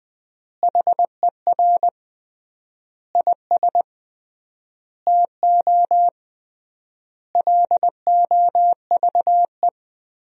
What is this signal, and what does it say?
Morse code, heard as a keyed tone, about 20 wpm, 700 Hz. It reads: HER IS TO LOVE
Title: 3x Word-Spacing - Sentences from Top 400 Words - 20wpm